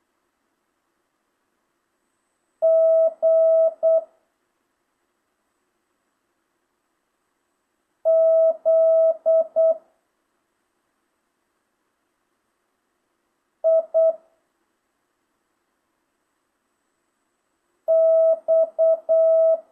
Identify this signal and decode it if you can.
Morse code, heard as a keyed tone, about 8 wpm, 650 Hz. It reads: G Z I X
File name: jh8cbh / 2025cw